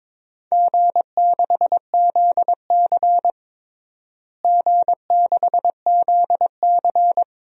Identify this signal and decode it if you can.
Morse code, heard as a keyed tone, about 22 wpm, 700 Hz. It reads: G6ZC G6ZC
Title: DXCC-100 Call Signs - 4 Characters - Repeated 2x - 22wpm